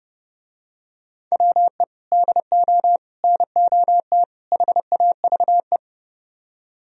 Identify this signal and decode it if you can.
Morse code, heard as a keyed tone, about 30 wpm, 700 Hz. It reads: WE DO NOT HAVE